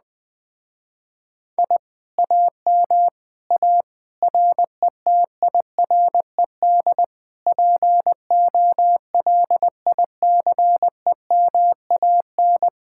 Morse code, heard as a keyed tone, about 20 wpm, 700 Hz. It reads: I AM A RETIRED POLICEMAN
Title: Speed-Racing Common QSO Phrases Encore - 17wpm